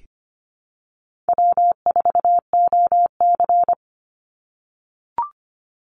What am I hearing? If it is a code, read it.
Morse code, heard as a keyed tone, about 25 wpm, 700 Hz. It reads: W4OC
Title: Easy Call Signs - 25wpm